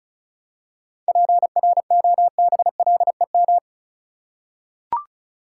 Morse code, heard as a keyed tone, about 34 wpm, 700 Hz. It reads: PROBLEM
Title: Seven Letter Words - 35wpm